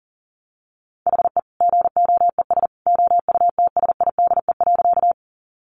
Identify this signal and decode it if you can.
Morse code, heard as a keyed tone, about 37 wpm, 700 Hz. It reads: HE GOES OUTSIDE.
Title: Additional Easy Phrases 40wpm